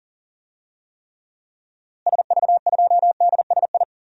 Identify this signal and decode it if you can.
Morse code, heard as a keyed tone, about 40 wpm, 700 Hz. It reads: SV2DSI
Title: DXCC-100 Call Signs - 6 Characters - Repeated 3x - 40wpm